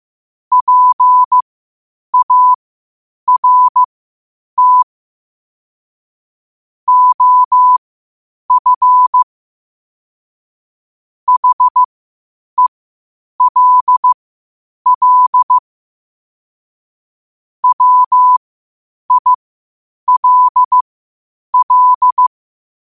Morse code, heard as a keyed tone, about 15 wpm, spaced as for 8 wpm, 1000 Hz. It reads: PART OF HELL WILL
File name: qotdfiles / 8wpm